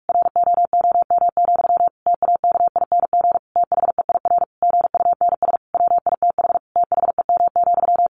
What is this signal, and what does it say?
Morse code, heard as a keyed tone, about 45 wpm, 700 Hz. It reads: ROOM, TAKING THEIR GUNS WITH THEM,
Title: A Sound Of Thunder 45wpm